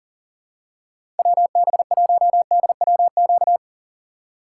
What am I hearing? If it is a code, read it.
Morse code, heard as a keyed tone, about 40 wpm, 700 Hz. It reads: WB1DWQ
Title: USA Call Signs - 6 Characters - 2x Character Spacing - 40wpm